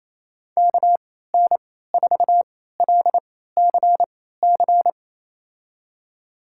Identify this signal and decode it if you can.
Morse code, heard as a keyed tone, about 28 wpm, spaced as for 16 wpm, 700 Hz. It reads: KN4LCC
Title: USA Call Signs - 6 Characters - 3x Character Spacing - 28wpm